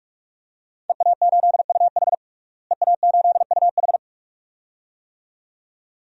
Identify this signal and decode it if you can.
Morse code, heard as a keyed tone, about 46 wpm, 700 Hz. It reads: EA8UH EA8UH
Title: DXCC-100 Call Signs - 5 Characters - Repeated 2x - 45wpm